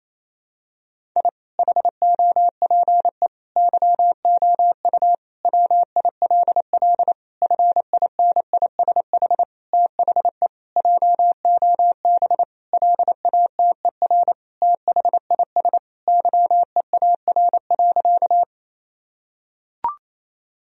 Morse code, heard as a keyed tone, about 28 wpm, 700 Hz. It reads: I HOPE YOU WILL FINISH THE JOB LATER THIS YEAR.
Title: Sentences from Top 1000 Words - Review Part 3 - 28wpm